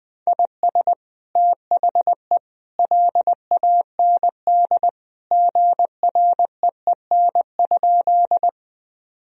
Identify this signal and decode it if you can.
Morse code, heard as a keyed tone, about 20 wpm, 700 Hz. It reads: IS THE LAND GREEN?
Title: Sentences from Top 1000 Words - Review Part 2 - 20wpm